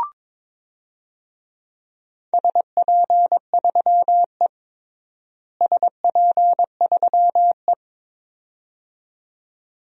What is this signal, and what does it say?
Morse code, heard as a keyed tone, about 22 wpm, 700 Hz. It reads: SP3E SP3E
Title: DXCC-100 Call Signs - 4 Characters - Repeated 2x - 22wpm